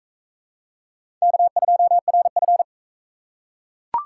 Morse code, heard as a keyed tone, about 37 wpm, 700 Hz. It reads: K2RF